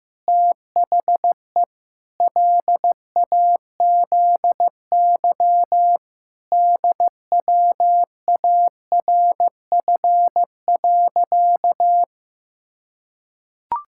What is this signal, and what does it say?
Morse code, heard as a keyed tone, about 15 wpm, 700 Hz. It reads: THE LAZY DWARF.
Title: Pangram Challenge v2 - 15wpm